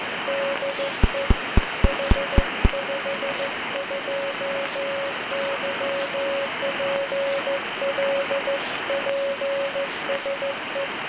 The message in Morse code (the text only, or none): DE H52YPL PSE